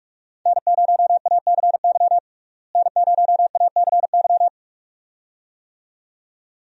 Morse code, heard as a keyed tone, about 41 wpm, 700 Hz. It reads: N0ACY N0ACY
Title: USA Call Signs - 5 Characters - Repeated 2x - 45wpm